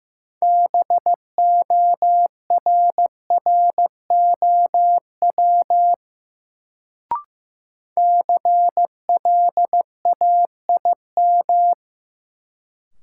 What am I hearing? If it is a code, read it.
Morse code, heard as a keyed tone, about 15 wpm, 700 Hz. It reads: BORROW CLAIM
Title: Sentences from Top 1500 Words - New Word Drill Alt - 15wpm